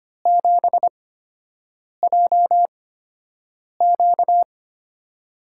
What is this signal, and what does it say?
Morse code, heard as a keyed tone, about 25 wpm, spaced as for 7 wpm, 700 Hz. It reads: 7JQ
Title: USA Call Signs - 5 Characters - 8x Character Spacing - 25wpm